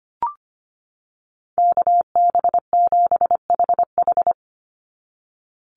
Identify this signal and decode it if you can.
Morse code, heard as a keyed tone, about 25 wpm, 700 Hz. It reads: KB7HH
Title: NA SOTA Chaser Callsigns 25wpm